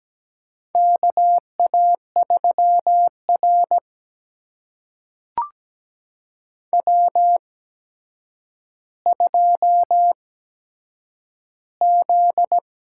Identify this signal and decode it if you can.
Morse code, heard as a keyed tone, about 17 wpm, spaced as for 8 wpm, 700 Hz. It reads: KA3R W2Z
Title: USA Call Signs - 4 Characters - 8x Character Spacing - 17wpm